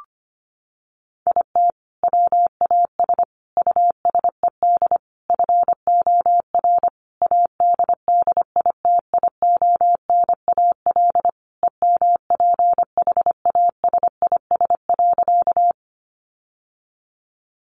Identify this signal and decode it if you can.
Morse code, heard as a keyed tone, about 25 wpm, 700 Hz. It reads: IT WAS USED FOR ADDITIONAL EMPHASIS.
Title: Sentences from Top 1400 Words 25wpm